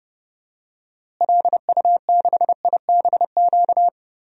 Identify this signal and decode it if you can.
Morse code, heard as a keyed tone, about 30 wpm, 700 Hz. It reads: LU6IBQ
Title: DXCC-100 Call Signs - 6 Characters - Repeated 2x - 30wpm